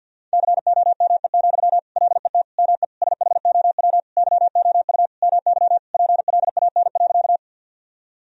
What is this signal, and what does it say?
Morse code, heard as a keyed tone, about 48 wpm, 700 Hz. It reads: COME, LET ME SHOW YOU MY PLAN.
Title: Sentences from Top 1000 Words - Review Part 2 - 50wpm